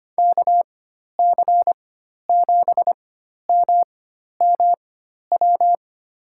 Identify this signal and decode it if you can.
Morse code, heard as a keyed tone, about 25 wpm, spaced as for 12 wpm, 700 Hz. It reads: KC7MMW
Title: USA Call Signs - 6 Characters - 4x Character Spacing - 25wpm